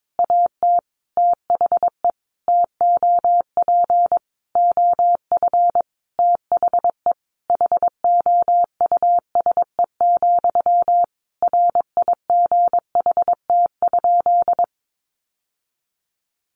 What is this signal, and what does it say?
Morse code, heard as a keyed tone, about 22 wpm, 700 Hz. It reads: AT THE TOP OF THE HOUSE, RIGHT?
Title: Sentences from Top 1200 Words 22wpm